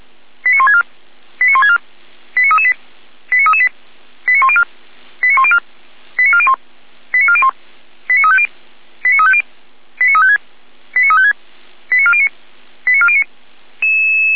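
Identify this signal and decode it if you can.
Five-tone selective call (ZVEI), heard as quick runs of five tones, tones of about 70 ms. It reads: 89164 89164 89338 89338 89104 89104 89401 89401 89350 89350 89356 89356 89409 89409
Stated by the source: ZVEI five-tone selective call 89701 received on SDR, fire brigade Feuerwehr Orlamünde.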